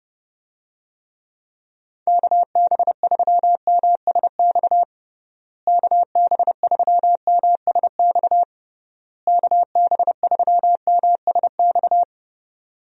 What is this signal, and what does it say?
Morse code, heard as a keyed tone, about 31 wpm, 700 Hz. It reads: KB3MSX KB3MSX KB3MSX
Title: USA Call Signs - 6 Characters - Repeated 3x - 30wpm